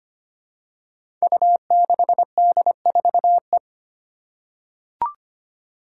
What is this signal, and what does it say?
Morse code, heard as a keyed tone, about 25 wpm, 700 Hz. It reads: U6D4E
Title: Five Challenging Letter-Number Combinations 25wpm